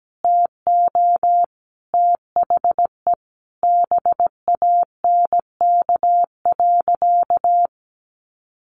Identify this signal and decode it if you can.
Morse code, heard as a keyed tone, about 17 wpm, 700 Hz. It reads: TO THE BANK.
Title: Sentences from Top 1500 Words 17wpm